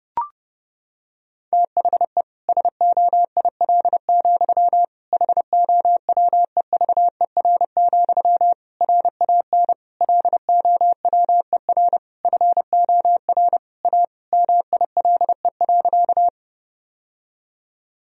Morse code, heard as a keyed tone, about 32 wpm, 700 Hz. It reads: THE SOIL, HOWEVER, RAN LOWER FOR A MILE.
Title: Sentences from Top 1000 Words - Review Part 4 - 30wpm